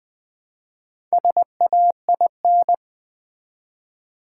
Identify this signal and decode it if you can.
Morse code, heard as a keyed tone, about 20 wpm, 700 Hz. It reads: SAIN
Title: Build A Word - 1 Letter At A Time - 3 to 5-Letter Words - 20wpm